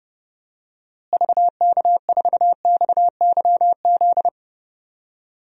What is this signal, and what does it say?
Morse code, heard as a keyed tone, about 32 wpm, 700 Hz. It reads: VK4XYZ
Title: DXCC-100 Call Signs - 6 Characters - Repeated 3x - 30wpm